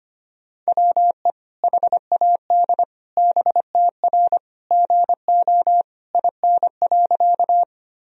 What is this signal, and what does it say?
Morse code, heard as a keyed tone, about 25 wpm, 700 Hz. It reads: WE HAD BTR GO IN.
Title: Sentences from Top 400 Words and Abbreviations 25wpm